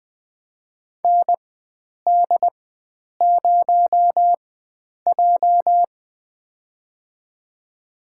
Morse code, heard as a keyed tone, about 20 wpm, spaced as for 9 wpm, 700 Hz. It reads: ND0J
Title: USA Call Signs - 4 Characters - 4x Character Spacing - 20wpm